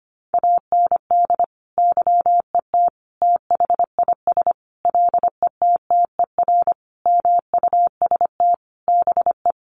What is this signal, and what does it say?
Morse code, heard as a keyed tone, about 25 wpm, 700 Hz. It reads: AND YET THIS LETTER MUST BE
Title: Sentences from Top 1400 Words 25wpm